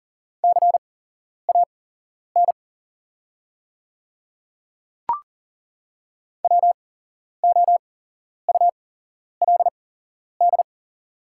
Morse code, heard as a keyed tone, about 40 wpm, spaced as for 11 wpm, 690 Hz. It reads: CAN WOULD
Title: 8x Character Spacing - Top 100 Words - 40wpm